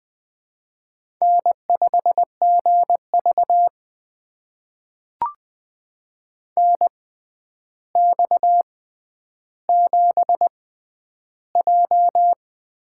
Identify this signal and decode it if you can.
Morse code, heard as a keyed tone, about 20 wpm, spaced as for 10 wpm, 700 Hz. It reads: N5GV NX7J
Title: USA Call Signs - 6x Character Spacing - 20wpm